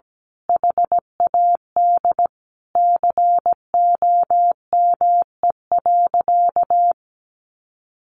Morse code, heard as a keyed tone, about 17 wpm, 700 Hz. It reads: HAD COME.
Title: Sentences from Top 1500 Words 17wpm